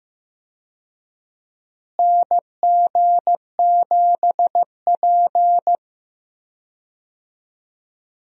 Morse code, heard as a keyed tone, about 15 wpm, 700 Hz. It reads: NG7P
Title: USA Call Signs - 4 Characters - Repeated 2x - 15wpm